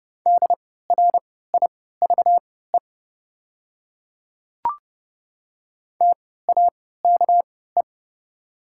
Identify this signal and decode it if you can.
Morse code, heard as a keyed tone, about 30 wpm, spaced as for 17 wpm, 690 Hz. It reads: DRIVE TAKE
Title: 3x Character Spacing - Top 500 Words - 30wpm